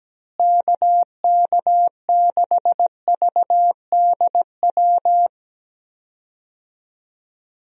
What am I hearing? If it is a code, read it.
Morse code, heard as a keyed tone, about 17 wpm, 700 Hz. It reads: KK6VDW
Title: USA Call Signs - 6 Characters - Repeated 3x - 17wpm